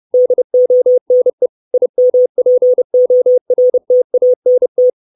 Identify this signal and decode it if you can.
Morse code, heard as a keyed tone, about 30 wpm, 500 Hz. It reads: DONE IMPORTANT
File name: 632 Words 30WPM